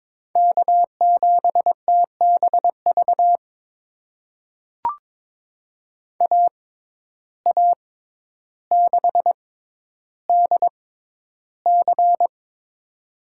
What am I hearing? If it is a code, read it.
Morse code, heard as a keyed tone, about 22 wpm, spaced as for 11 wpm, 700 Hz. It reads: K7TBV AA6DC
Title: USA Call Signs - 5 Characters - 6x Character Spacing - 22wpm